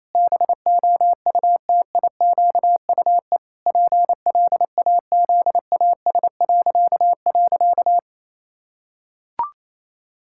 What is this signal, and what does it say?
Morse code, heard as a keyed tone, about 28 wpm, 700 Hz. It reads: BOUTIQUE PLAZAS..
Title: Pangram Challenge v2 - 28wpm